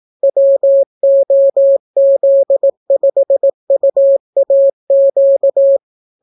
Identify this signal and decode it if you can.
Morse code, heard as a keyed tone, about 18 wpm, 550 Hz. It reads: WOZ5UAQ